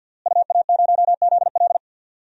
Morse code, heard as a keyed tone, about 50 wpm, 700 Hz. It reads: UA0ZL
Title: DXCC-100 Call Signs - 5 Characters - Repeated 3x - 50wpm